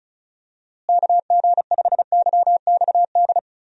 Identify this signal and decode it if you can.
Morse code, heard as a keyed tone, about 34 wpm, 700 Hz. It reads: KG5YXD
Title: USA Call Signs - 6 Characters - Repeated 3x - 35wpm